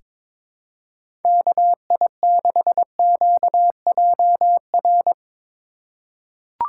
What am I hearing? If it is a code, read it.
Morse code, heard as a keyed tone, about 22 wpm, 700 Hz. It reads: KI6QJR
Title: USA Call Signs - 3x Character Spacing - 22wpm